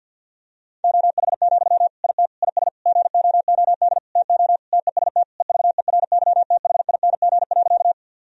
Morse code, heard as a keyed tone, about 50 wpm, 700 Hz. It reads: OH, IT IS GOOD TO TEST EVERYTHING.
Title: Sentences from Top 1000 Words - Review Part 3 - 50wpm